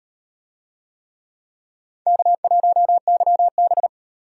Morse code, heard as a keyed tone, about 37 wpm, 700 Hz. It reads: K1YB